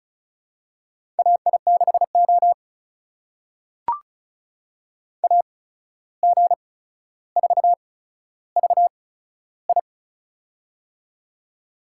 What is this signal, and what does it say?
Morse code, heard as a keyed tone, about 36 wpm, spaced as for 14 wpm, 700 Hz. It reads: AI6O AG4VI
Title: USA Call Signs - 8x Character Spacing - 35wpm